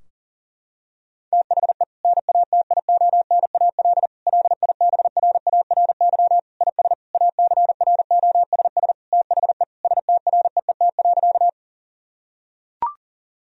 Morse code, heard as a keyed tone, about 37 wpm, 700 Hz. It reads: THE NATIONAL LIBRARY IS ACROSS THE STREET.
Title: Sentences from Top 1000 Words - Review Part 4 - 40wpm